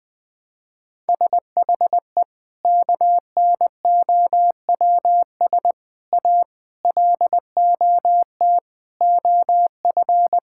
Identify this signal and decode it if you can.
Morse code, heard as a keyed tone, about 20 wpm, 700 Hz. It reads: SHE KNOWS A LOT OF
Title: Sentences with 4-Word Phrases - 20wpm